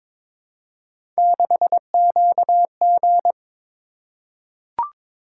Morse code, heard as a keyed tone, about 22 wpm, 700 Hz. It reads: 6QG
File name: practice / 22wpm / Three Letter-Number Combinations - 22wpm